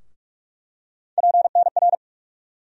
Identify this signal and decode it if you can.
Morse code, heard as a keyed tone, about 46 wpm, 700 Hz. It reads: PNR
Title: USA Call Sign Postfixes - 3 Letters 45wpm